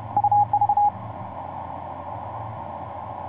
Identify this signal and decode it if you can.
Morse code, heard as a keyed tone, about 30 wpm, 820 Hz. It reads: UV
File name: z21dxi sp4ndu etc 17m cw 11feb2010 1404